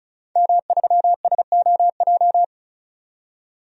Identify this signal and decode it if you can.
Morse code, heard as a keyed tone, about 36 wpm, 700 Hz. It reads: M3SOJ
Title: DXCC-100 Call Signs - 5 Characters - 35wpm